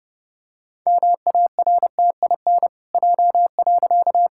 Morse code, heard as a keyed tone, about 32 wpm, 700 Hz. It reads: MARTIN J.
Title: Pangram Challenge v2 - 30wpm